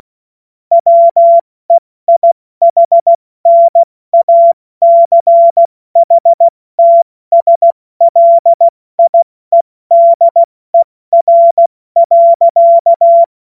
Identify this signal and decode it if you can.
Morse code, heard as a keyed tone, about 16 wpm, 680 Hz. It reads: WEIHNACHTSLIEDER.
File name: Morsekurs / Witze / Witz 16. November 2021, 70 F80